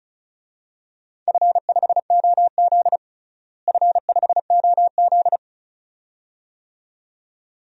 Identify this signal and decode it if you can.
Morse code, heard as a keyed tone, about 34 wpm, 700 Hz. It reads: F5OZ F5OZ
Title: DXCC-100 Call Signs - 4 Characters - Repeated 2x - 35wpm